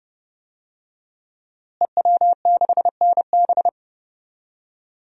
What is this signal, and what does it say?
Morse code, heard as a keyed tone, about 30 wpm, 700 Hz. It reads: EW6NB
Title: DXCC-100 Call Signs - 5 Characters - Repeated 3x - 30wpm